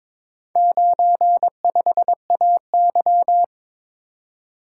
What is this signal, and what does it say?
Morse code, heard as a keyed tone, about 22 wpm, 700 Hz. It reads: 95AY
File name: practice / 22wpm / Four Letter-Number Combinations - 22wpm